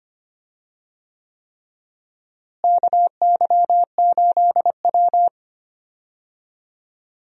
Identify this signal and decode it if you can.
Morse code, heard as a keyed tone, about 25 wpm, 700 Hz. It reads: KY8W
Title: USA Call Signs - 4 Characters - Repeated 3x - 25wpm